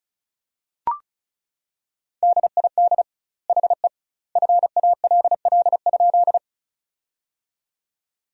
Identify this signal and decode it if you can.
Morse code, heard as a keyed tone, about 36 wpm, 700 Hz. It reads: DID HE FALL?
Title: 2x Word-Spacing - Sentences from Top 500 Words - 35wpm